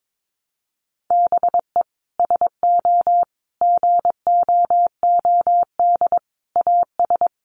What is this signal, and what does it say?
Morse code, heard as a keyed tone, about 22 wpm, 700 Hz. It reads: BE SO GOOD AS
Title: Sentences from Top 900 Words 22wpm